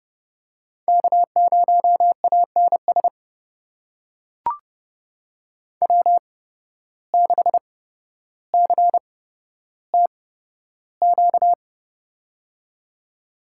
Morse code, heard as a keyed tone, about 30 wpm, spaced as for 13 wpm, 700 Hz. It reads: K0ANS W6CTQ
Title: USA Call Signs - 5 Characters - 8x Character Spacing - 30wpm